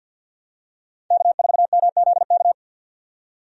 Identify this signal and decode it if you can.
Morse code, heard as a keyed tone, about 52 wpm, 700 Hz. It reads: K4MZK
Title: USA Call Signs - 5 Characters - 50wpm